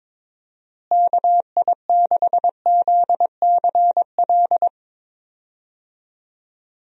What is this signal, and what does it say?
Morse code, heard as a keyed tone, about 22 wpm, 700 Hz. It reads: KI6ZCL
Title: USA Call Signs - 6 Characters - Repeated 2x - 22wpm